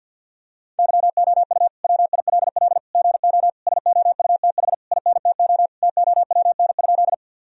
Morse code, heard as a keyed tone, about 50 wpm, 700 Hz. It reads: YOU WILL GO SOUTH INTO TOWN?
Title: Sentences from Top 1000 Words - Review Part 2 - 50wpm